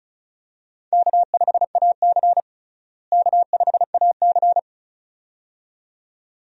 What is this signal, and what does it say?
Morse code, heard as a keyed tone, about 36 wpm, 700 Hz. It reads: K5AC K5AC
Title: USA Call Signs - 4 Characters - Repeated 2x - 35wpm